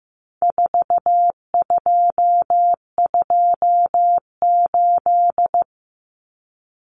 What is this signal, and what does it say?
Morse code, heard as a keyed tone, about 15 wpm, 700 Hz. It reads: 4228
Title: Numbers - 4-Digit 15wpm